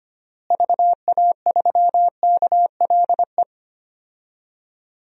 Morse code, heard as a keyed tone, about 25 wpm, 700 Hz. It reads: VA3KLE
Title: DXCC-100 Call Signs - 6 Characters - Repeated 3x - 25wpm